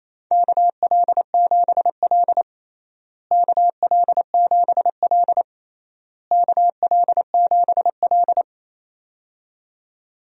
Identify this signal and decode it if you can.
Morse code, heard as a keyed tone, about 28 wpm, 700 Hz. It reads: KL7L KL7L KL7L
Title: DXCC-100 Call Signs - 4 Characters - Repeated 3x - 28wpm